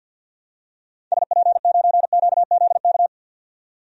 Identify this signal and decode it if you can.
Morse code, heard as a keyed tone, about 46 wpm, 700 Hz. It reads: SP9QZK
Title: DXCC-100 Call Signs - 6 Characters - 50wpm